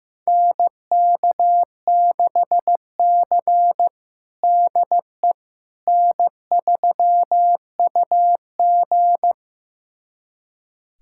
Morse code, heard as a keyed tone, about 15 wpm, 700 Hz. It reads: NK6C DE N3UG
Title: USA Callsign DE Callsign - 4 Characters - 15wpm